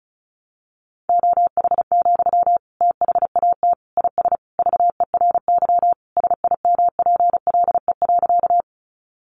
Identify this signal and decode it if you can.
Morse code, heard as a keyed tone, about 34 wpm, 700 Hz. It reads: OH, THAT IS VERY SIMPLE.
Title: Sentences from Top 500 Words 35wpm